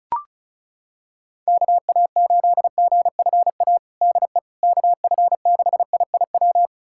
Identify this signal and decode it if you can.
Morse code, heard as a keyed tone, about 36 wpm, 700 Hz. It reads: KA8GFA DE KF6IIW
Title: USA Callsign DE Callsign 35wpm